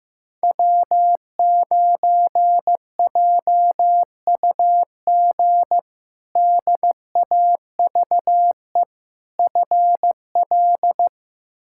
Morse code, heard as a keyed tone, about 15 wpm, 700 Hz. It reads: W9JUG DAVE FL
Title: K1USN SST QSO Practice - 15wpm